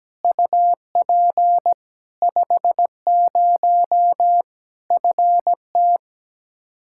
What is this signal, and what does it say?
Morse code, heard as a keyed tone, about 17 wpm, 700 Hz. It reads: UP 50 FT
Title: Practice QSO - Exchange 2 - Info - 17wpm